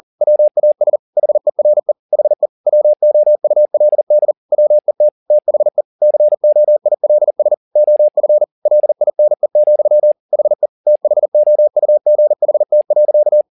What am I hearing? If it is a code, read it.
Morse code, heard as a keyed tone, about 40 wpm, 600 Hz. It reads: WAS HERE HE WOULD WET THE COILS OF LINE, HE THOUGHT.
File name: the old man and the sea 2 40wpm